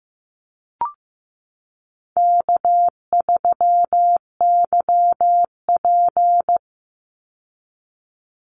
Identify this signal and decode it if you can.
Morse code, heard as a keyed tone, about 15 wpm, 700 Hz. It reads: K3YP
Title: Call Signs Repeated 2x - 15wpm